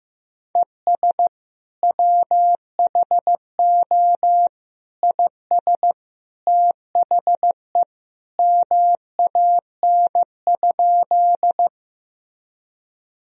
Morse code, heard as a keyed tone, about 15 wpm, 700 Hz. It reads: ES WHO IS THE MAN?
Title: Sentences from Top 200 Words and Abbreviations 15wpm